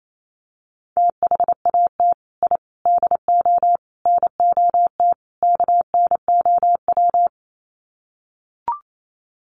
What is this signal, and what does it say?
Morse code, heard as a keyed tone, about 28 wpm, 700 Hz. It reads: THAT I DO NOT KNOW
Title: Sets of 5 Words Encore2 28wpm